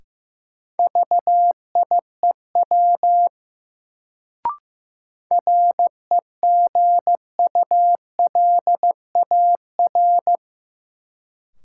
Morse code, heard as a keyed tone, about 15 wpm, 700 Hz. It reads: VIEW REGULAR
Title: Sentences from Top 900 Words - New Word Drill - 15wpm